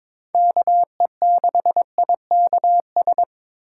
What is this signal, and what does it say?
Morse code, heard as a keyed tone, about 25 wpm, 700 Hz. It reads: KE6IKS
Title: Speed Racing - USA 2x3 Call Signs 22wpm